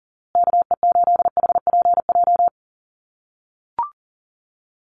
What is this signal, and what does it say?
Morse code, heard as a keyed tone, about 40 wpm, 700 Hz. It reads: KE8HPJ
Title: USA Call Signs Repeated 3x 40wpm